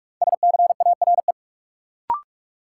Morse code, heard as a keyed tone, about 46 wpm, 700 Hz. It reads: SCARE
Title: Sentences from Top 1000 Words - Review Part 5 - New Word Drill - 45wpm